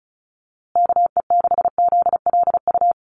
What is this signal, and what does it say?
Morse code, heard as a keyed tone, about 34 wpm, 700 Hz. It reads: KE6ZLU
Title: USA Call Signs Repeated 2x 35wpm